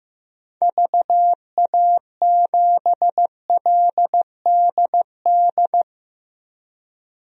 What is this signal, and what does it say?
Morse code, heard as a keyed tone, about 15 wpm, 700 Hz. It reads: VA7LDD
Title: DXCC-100 Call Signs - 6 Characters - 15wpm